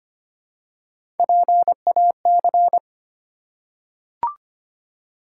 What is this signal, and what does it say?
Morse code, heard as a keyed tone, about 25 wpm, 700 Hz. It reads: PAC
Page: morse-code-ninja-practice-25wpm